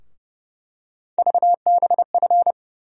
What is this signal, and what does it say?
Morse code, heard as a keyed tone, about 30 wpm, 700 Hz. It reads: VBF
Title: USA Call Sign Postfixes - 3 Letters 30wpm